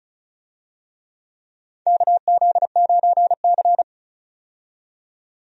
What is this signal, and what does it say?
Morse code, heard as a keyed tone, about 36 wpm, 700 Hz. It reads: KZ9C
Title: USA Call Signs - 4 Characters - 35wpm